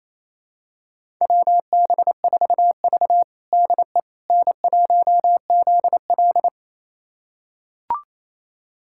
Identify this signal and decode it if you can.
Morse code, heard as a keyed tone, about 29 wpm, 690 Hz. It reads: WB4V DE N1ZL
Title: USA Callsign DE Callsign - 4 Characters - 28wpm